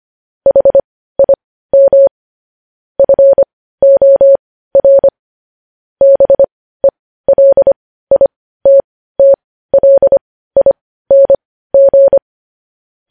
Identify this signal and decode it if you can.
Morse code, heard as a keyed tone, about 25 wpm, spaced as for 15 wpm, 550 Hz. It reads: HIM FOR BELITTLING
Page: News headlines in Morse code at 15 wpm